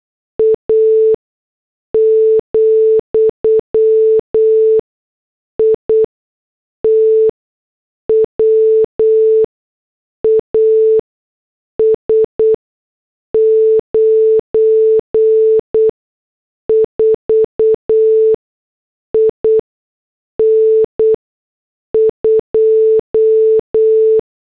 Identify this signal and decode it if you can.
Morse code, heard as a keyed tone, about 8 wpm, spaced as for 6 wpm, 440 Hz. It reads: A, IT WAS 94 IN 2
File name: morse / headlines